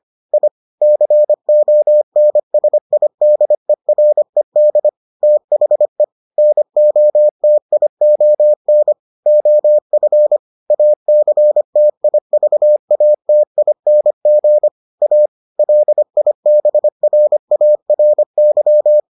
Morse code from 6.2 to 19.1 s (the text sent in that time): NOTION OF ACTIVATING A LIBRARY